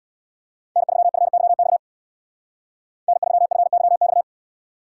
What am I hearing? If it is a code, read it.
Morse code, heard as a keyed tone, about 62 wpm, 700 Hz. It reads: N3FYX N3FYX
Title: Speed Racing - USA 1x3 Call Signs 50wpm